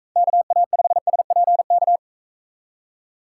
Morse code, heard as a keyed tone, about 42 wpm, 700 Hz. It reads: KA5SPK